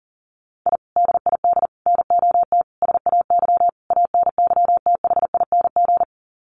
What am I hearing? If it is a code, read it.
Morse code, heard as a keyed tone, about 40 wpm, 700 Hz. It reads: I DID NOT SAY ANYTHING
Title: Sets of 5 Words Encore2 40wpm